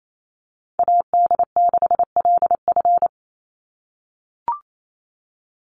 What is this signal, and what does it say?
Morse code, heard as a keyed tone, about 28 wpm, 700 Hz. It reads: AD6LF
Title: USA Call Signs 28wpm